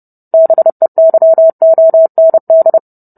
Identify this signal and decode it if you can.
Morse code, heard as a keyed tone, about 30 wpm, 650 Hz. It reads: BEYOND